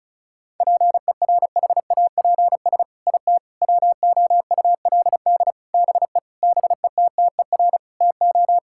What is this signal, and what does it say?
Morse code, heard as a keyed tone, about 36 wpm, 700 Hz. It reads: PERHAPS IT WOULD BE BETTER TO
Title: Sentences from Top 1600 Words 35wpm